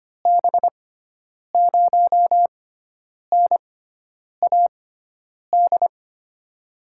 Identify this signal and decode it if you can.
Morse code, heard as a keyed tone, about 25 wpm, spaced as for 9 wpm, 700 Hz. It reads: B0NAD
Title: USA Call Signs - 6 Characters - 6x Character Spacing - 25wpm